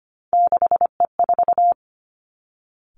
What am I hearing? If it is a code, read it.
Morse code, heard as a keyed tone, about 25 wpm, 700 Hz. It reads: 6E4
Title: Three Challenging Letter-Number Combinations 25wpm